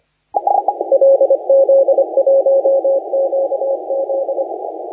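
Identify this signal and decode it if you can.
Morse code, heard as a keyed tone, about 25 wpm, 540 Hz. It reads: LZ1QZ
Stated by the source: Morse contest audio recorded by LA4C during the SAC CW 2018 contest.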